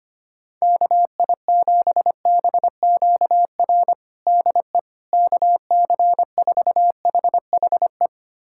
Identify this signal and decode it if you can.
Morse code, heard as a keyed tone, about 25 wpm, 700 Hz. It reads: KI7BQR DE KC4HHE
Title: USA Callsign DE Callsign 25wpm